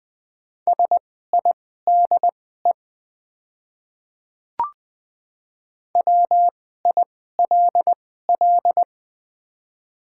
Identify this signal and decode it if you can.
Morse code, heard as a keyed tone, about 20 wpm, spaced as for 14 wpm, 690 Hz. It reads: SIDE WILL
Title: 2x Character Spacing - Top 300 Words - 20wpm